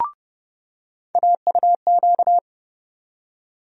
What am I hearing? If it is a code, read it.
Morse code, heard as a keyed tone, about 30 wpm, 700 Hz. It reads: AUQ
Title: USA Call Sign Postfixes - 3 Letters 30wpm